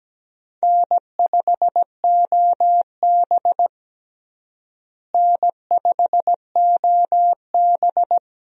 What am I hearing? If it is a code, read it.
Morse code, heard as a keyed tone, about 17 wpm, 700 Hz. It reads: N5OB N5OB
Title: USA Call Signs - 4 Characters - Repeated 2x - 17wpm